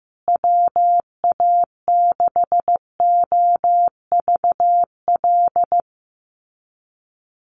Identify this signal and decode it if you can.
Morse code, heard as a keyed tone, about 15 wpm, 700 Hz. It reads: WA6OVL
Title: USA Call Signs Repeated 2x 15wpm